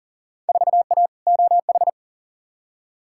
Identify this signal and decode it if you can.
Morse code, heard as a keyed tone, about 40 wpm, 700 Hz. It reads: 4A OH
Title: Field Day Exchanges - 40wpm